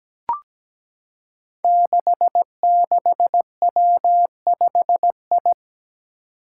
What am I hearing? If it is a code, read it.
Morse code, heard as a keyed tone, about 17 wpm, 700 Hz. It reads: 66W5I
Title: Five Challenging Letter-Number Combinations 17wpm